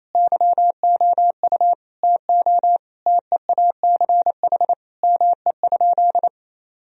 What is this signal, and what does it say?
Morse code, heard as a keyed tone, about 27 wpm, 690 Hz. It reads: YOU TO TEACH ME?
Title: Sentences from Top 1000 Words - Review Part 2 - 28wpm